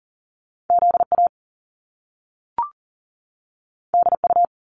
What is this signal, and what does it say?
Morse code, heard as a keyed tone, about 38 wpm, 700 Hz. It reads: ZA DU
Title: Top Two Letter Combinations in Words 40wpm